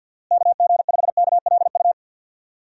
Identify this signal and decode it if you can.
Morse code, heard as a keyed tone, about 49 wpm, 700 Hz. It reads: KG5KLU
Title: USA Call Signs - 6 Characters - Repeated 2x - 50wpm